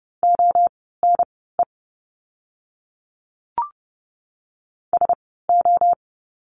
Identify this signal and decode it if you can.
Morse code, heard as a keyed tone, about 30 wpm, spaced as for 17 wpm, 700 Hz. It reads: ONE SO
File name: course / 30:10wpm / 230 Lesson 37f - Letter x - Words 30wpm